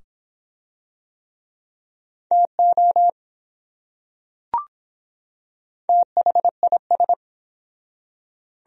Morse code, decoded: TO THIS

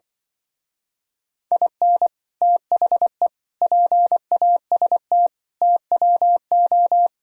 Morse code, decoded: IN THE PAST TWO

